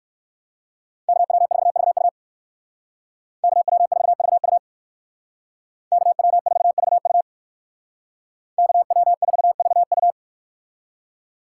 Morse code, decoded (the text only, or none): KW4VU KW4VU KW4VU KW4VU